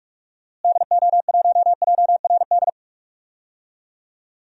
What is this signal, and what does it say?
Morse code, heard as a keyed tone, about 46 wpm, 700 Hz. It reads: DO1JRD
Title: DXCC-100 Call Signs - 6 Characters - Repeated 3x - 45wpm